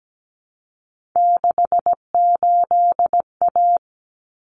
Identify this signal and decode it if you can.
Morse code, heard as a keyed tone, about 17 wpm, 700 Hz. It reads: 68A
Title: Common Cut Numbers - 3-Digit 17wpm